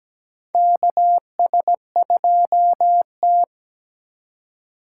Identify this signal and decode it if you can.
Morse code, heard as a keyed tone, about 17 wpm, 700 Hz. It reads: KS2T
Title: USA Call Signs - 4 Characters - 4x Character Spacing - 17wpm